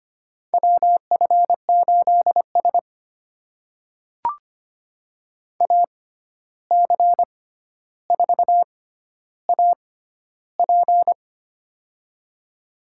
Morse code, heard as a keyed tone, about 25 wpm, spaced as for 12 wpm, 700 Hz. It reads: WF8S AC4AP
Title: USA Call Signs - 6x Character Spacing - 25wpm